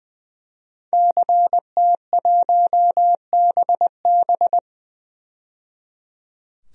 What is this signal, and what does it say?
Morse code, heard as a keyed tone, about 20 wpm, 700 Hz. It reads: CT1BB
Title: DXCC-100 Call Signs - 5 Characters - 20wpm